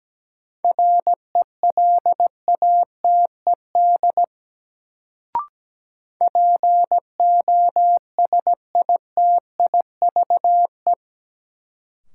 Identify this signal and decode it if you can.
Morse code, heard as a keyed tone, about 17 wpm, 700 Hz. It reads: RELATED POSITIVE
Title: Sentences from Top 1000 Words - Review Part 5 - New Word Drill - 17wpm